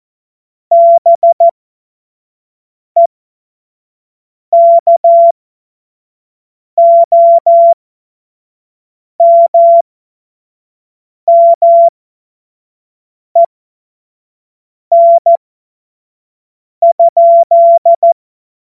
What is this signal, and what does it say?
Morse code, heard as a keyed tone, about 14 wpm, spaced as for 5 wpm, 680 Hz. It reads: BEKOMMEN?
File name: Morsekurs / Witze / Witz 15. August 2023 25 F70